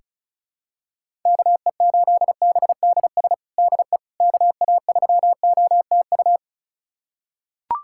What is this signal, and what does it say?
Morse code, heard as a keyed tone, about 34 wpm, 690 Hz. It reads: KE8BDS DE KA3OTU
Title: USA Callsign DE Callsign 35wpm